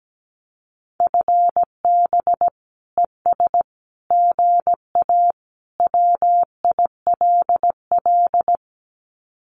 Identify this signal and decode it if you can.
Morse code, heard as a keyed tone, about 17 wpm, 700 Hz. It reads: FB ES GA WILL
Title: Common QSO Phrases Encore 17wpm